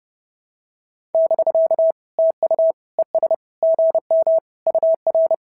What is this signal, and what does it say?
Morse code, heard as a keyed tone, about 30 wpm, 650 Hz. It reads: <BK> TU ES GM UR
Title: SOTA QSO Practice 30wpm